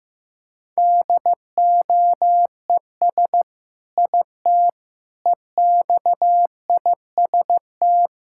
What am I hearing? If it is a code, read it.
Morse code, heard as a keyed tone, about 16 wpm, 700 Hz. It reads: DOES IT EXIST
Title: Sentences from Top 1000 Words - Review Part 5 - 15wpm